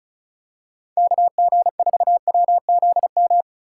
Morse code, heard as a keyed tone, about 35 wpm, 700 Hz. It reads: KG4WZM
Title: USA Call Signs - 6 Characters - 8x Character Spacing - 35wpm